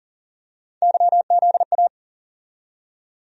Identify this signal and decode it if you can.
Morse code, heard as a keyed tone, about 40 wpm, 700 Hz. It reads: YZA
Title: USA Call Sign Postfixes - 3 Letters 40wpm